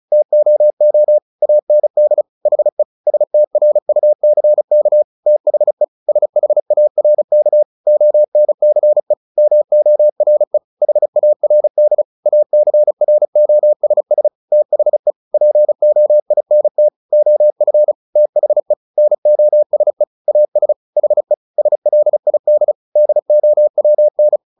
Morse code, heard as a keyed tone, about 35 wpm, 600 Hz. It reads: TOO AND HE STRUCK THE SHARK ONCE MORE HARD ACROSS THE POINT OF THE NOSE AS HE SLID DOWN
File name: the old man and the sea 2 35wpm